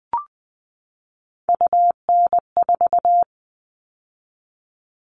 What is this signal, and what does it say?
Morse code, heard as a keyed tone, about 20 wpm, 700 Hz. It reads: UN4
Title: All Cut Numbers - 3-Digit 20wpm